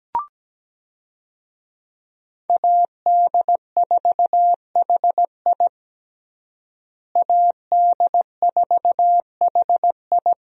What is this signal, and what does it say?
Morse code, heard as a keyed tone, about 17 wpm, 700 Hz. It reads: AD4HI AD4HI
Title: USA Call Signs - 5 Characters - Repeated 3x - 17wpm